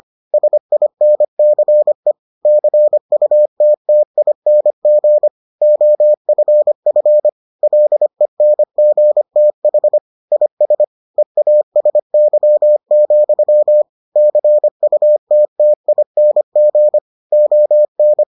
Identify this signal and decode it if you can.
Morse code, heard as a keyed tone, about 25 wpm, 600 Hz. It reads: SINCE CUTTING OFF LENGTH IS EASY, CUTTING ON